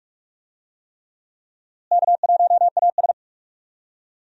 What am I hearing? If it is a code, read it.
Morse code, heard as a keyed tone, about 46 wpm, 700 Hz. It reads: K1AS
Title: USA Call Signs - 4 Characters - Repeated 3x - 45wpm